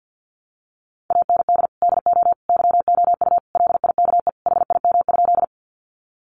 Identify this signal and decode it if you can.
Morse code, heard as a keyed tone, about 48 wpm, 700 Hz. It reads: AND DO YOU LIKE HIM?
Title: Sentences from Top 100 Words 50wpm